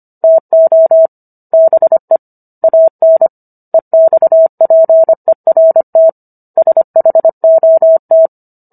Morse code, heard as a keyed tone, about 25 wpm, 650 Hz. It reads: TO BE AN EXPERT SHOT